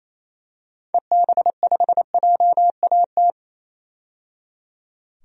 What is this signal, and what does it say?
Morse code, heard as a keyed tone, about 28 wpm, 700 Hz. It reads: EB5JAT
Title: DXCC-100 Call Signs - 6 Characters - 28wpm